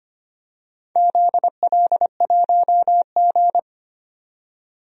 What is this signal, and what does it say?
Morse code, heard as a keyed tone, about 25 wpm, 700 Hz. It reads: ZL1G
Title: DXCC-100 Call Signs - 4 Characters - 25wpm